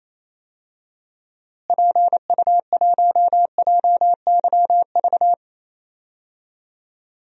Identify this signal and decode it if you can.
Morse code, heard as a keyed tone, about 27 wpm, 700 Hz. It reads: PU1JYV
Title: DXCC-100 Call Signs - 6 Characters - Repeated 2x - 28wpm